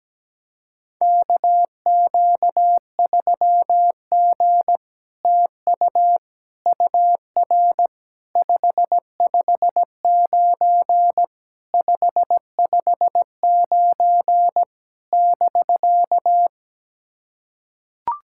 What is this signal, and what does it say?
Morse code, heard as a keyed tone, about 17 wpm, 700 Hz. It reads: KQ3G TU UR 559 559 <BK>
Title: SOTA QSO Practice 17wpm